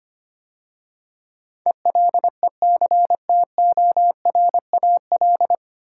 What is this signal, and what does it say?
Morse code, heard as a keyed tone, about 25 wpm, 700 Hz. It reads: ELECTORAL
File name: practice / 25wpm / 9 Letter Words - First 3 Letters Given - 25wpm